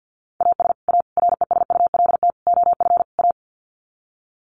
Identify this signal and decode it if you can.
Morse code, heard as a keyed tone, about 50 wpm, 700 Hz. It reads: AS A RESULT OF A
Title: Sets of 5 Words Encore 50wpm